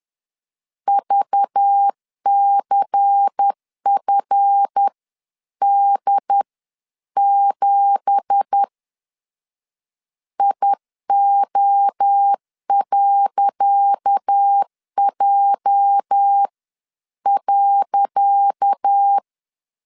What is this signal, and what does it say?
Morse code, heard as a keyed tone, about 10 wpm, spaced as for 9 wpm, 800 Hz. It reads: VCFD7 IO.J.
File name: LEZIONI / 10 SIGNS / 10C DOT 10wpm 2940